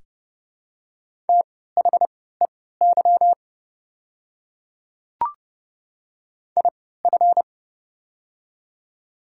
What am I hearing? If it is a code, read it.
Morse code, heard as a keyed tone, about 30 wpm, spaced as for 17 wpm, 700 Hz. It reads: THEY IF
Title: 3x Character Spacing - Top 100 Words - 30wpm